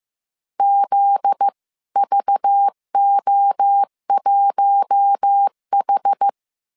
Morse code, heard as a keyed tone, about 15 wpm, 800 Hz. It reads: ZVO1H
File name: CW / DET / 1 9 1500C S C 14wpm 3245